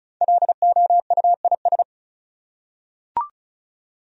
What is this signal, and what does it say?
Morse code, heard as a keyed tone, about 36 wpm, 700 Hz. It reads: LOUIS
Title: USA Common Names - Max 5 Letters - 35wpm